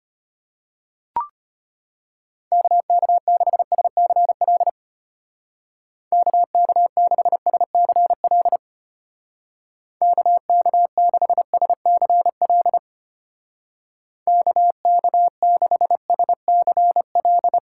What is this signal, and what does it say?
Morse code, heard as a keyed tone, about 33 wpm, 700 Hz. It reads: KK6SCL KK6SCL KK6SCL KK6SCL